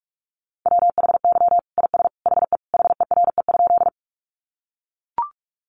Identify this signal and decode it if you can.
Morse code, heard as a keyed tone, about 44 wpm, 700 Hz. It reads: WHY IS HE HERE?
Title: Sentences from Top 200 Words 45wpm